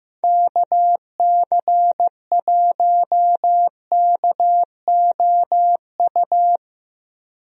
Morse code, heard as a keyed tone, about 15 wpm, 700 Hz. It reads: KC1KOU
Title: USA Call Signs - 6 Characters - Repeated 3x - 15wpm